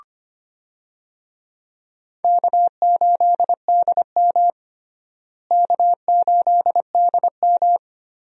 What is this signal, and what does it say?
Morse code, heard as a keyed tone, about 25 wpm, 700 Hz. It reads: K8DM K8DM
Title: USA Call Signs - 4 Characters - Repeated 2x - 25wpm